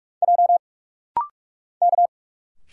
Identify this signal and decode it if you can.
Morse code, heard as a keyed tone, about 46 wpm, 700 Hz. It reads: J K
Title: Single Letters - Rapid-Fire 45wpm